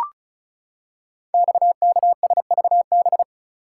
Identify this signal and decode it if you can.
Morse code, heard as a keyed tone, about 36 wpm, 690 Hz. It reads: XKSVB
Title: Code Groups - 5 Letters - 35wpm